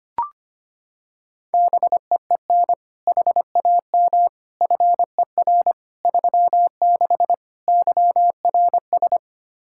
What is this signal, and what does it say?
Morse code, heard as a keyed tone, about 25 wpm, 700 Hz. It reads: BEEN HAM FER 36 YRS
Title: Practice QSO - Exchange 3 - More Info - 25wpm